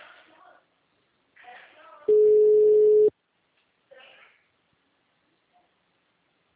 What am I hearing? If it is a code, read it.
Ringback tone (Europe, 425 Hz), with speech.